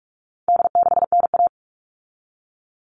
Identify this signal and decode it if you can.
Morse code, heard as a keyed tone, about 44 wpm, 700 Hz. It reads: D6NA
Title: All Cut Numbers - 4-Digit 45wpm